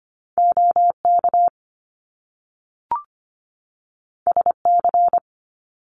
Morse code, heard as a keyed tone, about 25 wpm, 700 Hz. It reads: OK SC
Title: US States 25wpm